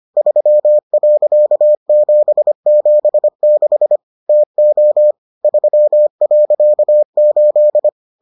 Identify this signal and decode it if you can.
Morse code, heard as a keyed tone, about 25 wpm, 600 Hz. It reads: 3.776 TO 3.8